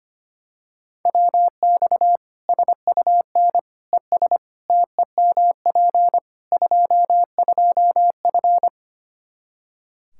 Morse code, heard as a keyed tone, about 25 wpm, 700 Hz. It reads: WX SUN ES TEMP 22F
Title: Practice QSO - Exchange 2 - Info - 25wpm